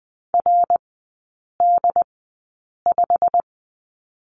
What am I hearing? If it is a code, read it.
Morse code, heard as a keyed tone, about 20 wpm, spaced as for 8 wpm, 700 Hz. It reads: RD5
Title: 087 Lesson 15e - Number 5 - Three Characters 20wpm